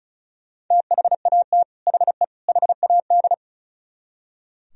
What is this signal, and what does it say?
Morse code, heard as a keyed tone, about 34 wpm, 700 Hz. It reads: THAT HE HAD
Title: Speed-Racing - Sets of 3 Words 35wpm